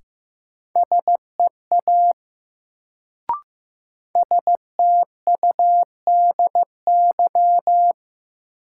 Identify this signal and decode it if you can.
Morse code, heard as a keyed tone, about 15 wpm, 700 Hz. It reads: SEA STUDY
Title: Sentences from Top 300 Words - New Word Drill - 15wpm